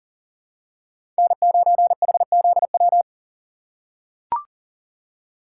Morse code, heard as a keyed tone, about 40 wpm, 700 Hz. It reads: N9HZW
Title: USA Call Signs - 5 Characters - 6x Character Spacing - 40wpm